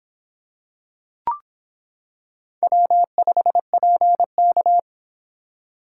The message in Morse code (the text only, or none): W5PK